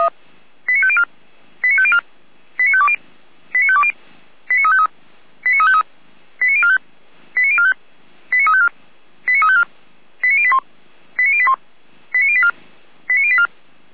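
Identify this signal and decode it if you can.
Five-tone selective call (ZVEI), heard as quick runs of five tones, tones of about 70 ms. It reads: 89504 89504 89520 89520 89353 89353 89045 89045 89354 89354 89081 89081 89084 89084
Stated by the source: ZVEI five-tone selective call received on SDR.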